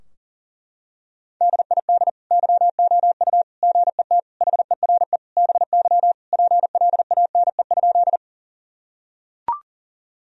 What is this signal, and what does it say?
Morse code, heard as a keyed tone, about 40 wpm, 700 Hz. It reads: DID YOU GET HERE BY PLANE?
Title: Sentences from Top 1000 Words - Review Part 3 - 40wpm